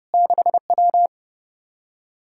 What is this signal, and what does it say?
Morse code, heard as a keyed tone, about 33 wpm, 700 Hz. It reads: 6W